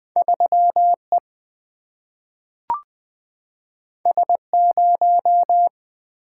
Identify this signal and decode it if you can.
Morse code, heard as a keyed tone, about 20 wpm, 700 Hz. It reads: 3E S0
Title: Single-Letter Single-Number 20wpm